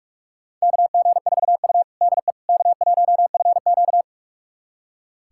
Morse code, heard as a keyed tone, about 41 wpm, 700 Hz. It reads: KG4U DE K1FQ
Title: USA Callsign DE Callsign - 4 Characters - 45wpm